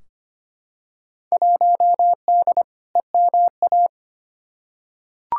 Morse code, heard as a keyed tone, about 25 wpm, 700 Hz. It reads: 1D EMA
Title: Field Day Exchanges - 25wpm